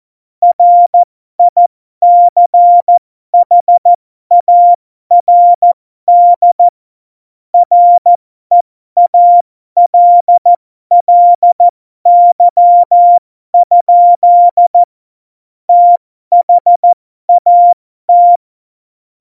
Morse code, heard as a keyed tone, about 14 wpm, spaced as for 12 wpm, 700 Hz. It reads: RICHARD REALLY? THAT
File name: Morsekurs / Witze